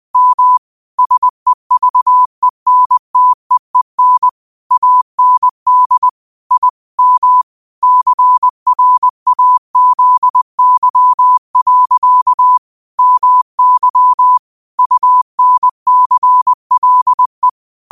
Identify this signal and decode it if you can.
Morse code, heard as a keyed tone, about 20 wpm, 1000 Hz. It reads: M SEVENTEEN AND I M CRAZY. MY UNCLE
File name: 20wpm